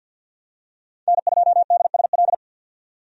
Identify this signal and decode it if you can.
Morse code, heard as a keyed tone, about 50 wpm, 700 Hz. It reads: N2DSL
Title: USA Call Signs - 5 Characters - 50wpm